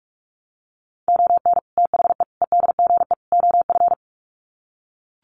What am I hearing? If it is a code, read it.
Morse code, heard as a keyed tone, about 44 wpm, 700 Hz. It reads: ON THE EDGE OF
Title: Sets of 4 Words 45wpm